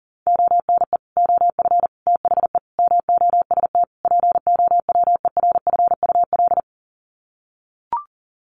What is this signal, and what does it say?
Morse code, heard as a keyed tone, about 40 wpm, 700 Hz. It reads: ONE OF THE MOST POWERFUL
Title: Sets of 5 Words Encore 40wpm